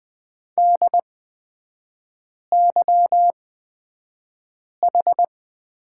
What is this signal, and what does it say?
Morse code, heard as a keyed tone, about 20 wpm, spaced as for 5 wpm, 700 Hz. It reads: DYH
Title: 236 Lesson 38e - Letter q - Three Characters 20wpm